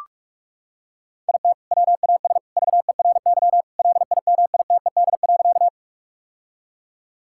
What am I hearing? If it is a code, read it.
Morse code, heard as a keyed tone, about 41 wpm, 700 Hz. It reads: IT WAS VERY LIMITED.
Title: Sentences from Top 1000 Words - Review Part 6 - 45wpm